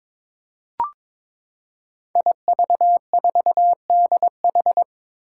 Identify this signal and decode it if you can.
Morse code, heard as a keyed tone, about 22 wpm, 700 Hz. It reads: IV4DH
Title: Five Challenging Letter-Number Combinations 22wpm